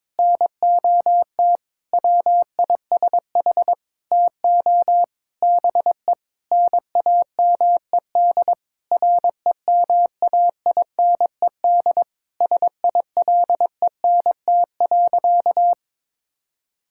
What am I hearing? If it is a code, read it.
Morse code, heard as a keyed tone, about 22 wpm, 700 Hz. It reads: NOT WISH TO BE NAMED REMAINED SILENT.
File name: Sentences with 7-Word Phrases - 22wpm